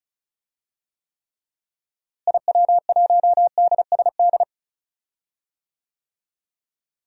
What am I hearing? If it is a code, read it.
Morse code, heard as a keyed tone, about 34 wpm, 700 Hz. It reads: IW1DSD
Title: DXCC-100 Call Signs - 6 Characters - Repeated 2x - 35wpm